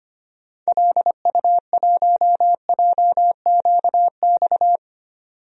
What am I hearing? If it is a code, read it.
Morse code, heard as a keyed tone, about 25 wpm, 700 Hz. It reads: LU1JQX
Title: DXCC-100 Call Signs - 6 Characters - 25wpm